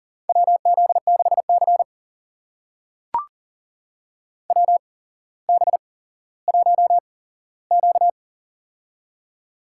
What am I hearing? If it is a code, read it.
Morse code, heard as a keyed tone, about 40 wpm, spaced as for 17 wpm, 700 Hz. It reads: WZ6C WB1Q
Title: USA Call Signs - 4 Characters - 8x Character Spacing - 40wpm